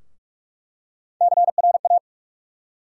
Morse code, heard as a keyed tone, about 44 wpm, 700 Hz. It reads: CRA
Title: USA Call Sign Postfixes - 3 Letters 45wpm